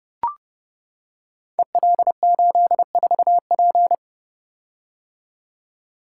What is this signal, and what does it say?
Morse code, heard as a keyed tone, about 31 wpm, 700 Hz. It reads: EL84P
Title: Code Groups - 5 Letter:Numbers - 30wpm